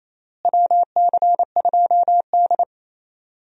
Morse code, heard as a keyed tone, about 27 wpm, 700 Hz. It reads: WC2D
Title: USA Call Signs - 4 Characters - Repeated 3x - 28wpm